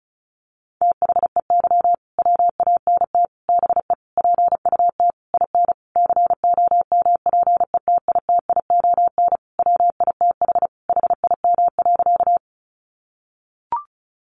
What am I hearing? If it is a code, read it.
Morse code, heard as a keyed tone, about 34 wpm, 700 Hz. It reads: THEY WANT BE PUT IN COMPETITION WITH HIM.
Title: Sentences from Top 1000 Words 35wpm